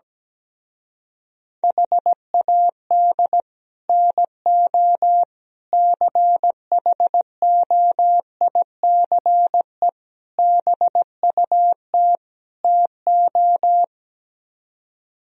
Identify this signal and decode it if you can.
Morse code, heard as a keyed tone, about 17 wpm, 700 Hz. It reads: HAD NO CHOICE BUT TO
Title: Speed-Racing - Sets of 5 Words 17wpm